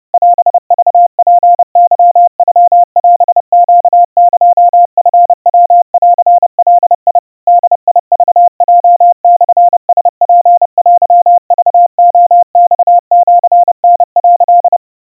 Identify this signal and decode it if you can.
Morse code (Japanese Wabun code), heard as a keyed tone, about 30 wpm, 700 Hz. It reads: オクツケノオネエチヤンガ ボクヲモラツテクレマシタ」
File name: 110 cpm 700 Hz Crispy